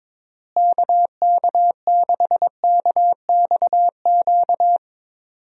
Morse code, heard as a keyed tone, about 22 wpm, 700 Hz. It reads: KK6KXQ
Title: USA Call Signs - 6 Characters - Repeated 2x - 22wpm